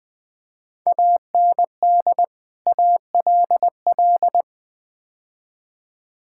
Morse code, heard as a keyed tone, about 20 wpm, 700 Hz. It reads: AND ALL
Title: Build A Phrase - 5-Word Phrases - 20wpm